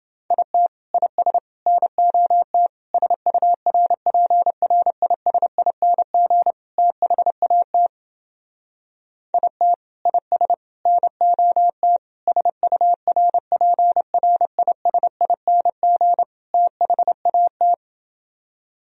Morse code, heard as a keyed tone, about 29 wpm, 700 Hz. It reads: IT IS NOT SURPRISING THAT IT IS NOT SURPRISING THAT